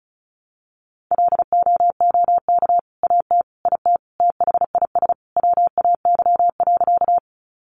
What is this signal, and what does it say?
Morse code, heard as a keyed tone, about 34 wpm, 700 Hz. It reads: LOOK AT IT THIS WAY.
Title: Sentences from Top 100 Words 35wpm